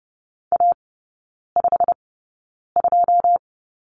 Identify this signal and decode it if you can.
Morse code, heard as a keyed tone, about 30 wpm, spaced as for 9 wpm, 700 Hz. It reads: A52
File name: Lesson 16e - Letter c - Three Characters 30wpm